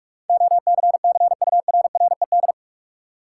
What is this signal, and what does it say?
Morse code, heard as a keyed tone, about 41 wpm, 700 Hz. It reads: OCCURRED
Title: Sets of 5 Words Encore - New Word Drill - 45wpm